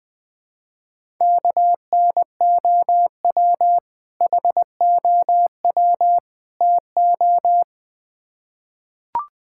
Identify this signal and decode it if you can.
Morse code, heard as a keyed tone, about 20 wpm, 700 Hz. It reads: KNOW HOW TO
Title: Build A Phrase - 3-Word Phrases - 20wpm